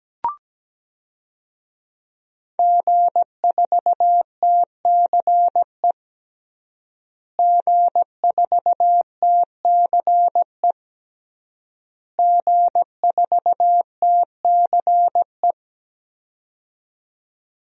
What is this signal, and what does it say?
Morse code, heard as a keyed tone, about 17 wpm, 700 Hz. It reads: G4TCE G4TCE G4TCE
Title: DXCC-100 Call Signs - 5 Characters - Repeated 3x - 17wpm